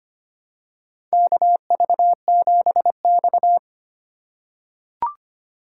Morse code, heard as a keyed tone, about 25 wpm, 700 Hz. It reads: KV7X
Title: USA Call Signs - 4 Characters - 3x Character Spacing - 25wpm